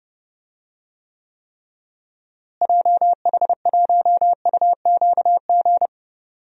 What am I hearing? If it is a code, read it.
Morse code, heard as a keyed tone, about 29 wpm, 700 Hz. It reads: JH1UQG
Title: DXCC-100 Call Signs - 6 Characters - Repeated 3x - 30wpm